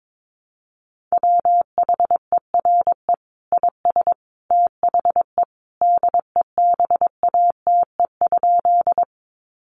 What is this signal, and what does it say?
Morse code, heard as a keyed tone, about 22 wpm, 700 Hz. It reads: WHERE IS THE DEBATE?
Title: Sentences from Top 1100 Words 22wpm